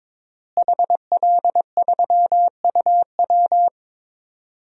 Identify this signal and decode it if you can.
Morse code, heard as a keyed tone, about 22 wpm, 700 Hz. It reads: HL3UW
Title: DXCC-100 Call Signs - 5 Characters - Repeated 2x - 22wpm